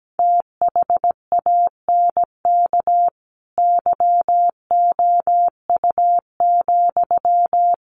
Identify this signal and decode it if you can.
Morse code, heard as a keyed tone, about 17 wpm, 700 Hz. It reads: THANK YOU,
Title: Sentences from Top 1200 Words 17wpm